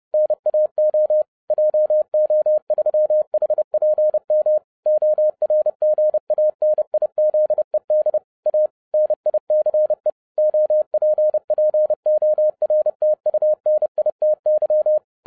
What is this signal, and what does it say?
Morse code, heard as a keyed tone, about 30 wpm, 610 Hz. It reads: NAO JO3HPM ORGANIZED A NICE OPPORTUNITY